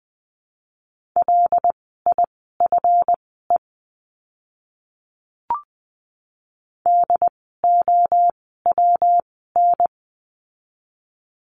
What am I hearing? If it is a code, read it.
Morse code, heard as a keyed tone, about 20 wpm, spaced as for 14 wpm, 700 Hz. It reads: LIFE DOWN
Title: 192 Lesson 31f - Letter v - Words 20wpm